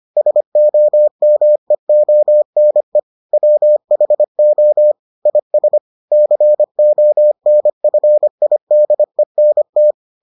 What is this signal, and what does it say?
Morse code, heard as a keyed tone, about 25 wpm, 600 Hz. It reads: SOMEONE WHO IS CONFIDENT